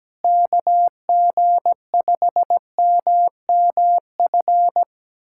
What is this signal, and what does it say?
Morse code, heard as a keyed tone, about 17 wpm, 700 Hz. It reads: KG5MMF
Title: USA Callsign DE Callsign - 6 Characters - 17wpm